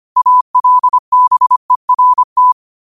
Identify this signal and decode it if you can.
Morse code, heard as a keyed tone, about 25 wpm, 1000 Hz. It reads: ALBERT